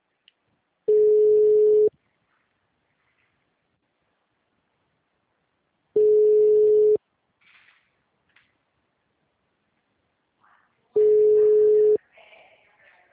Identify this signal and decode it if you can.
Ringback tone (Europe, 425 Hz).
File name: out- -2018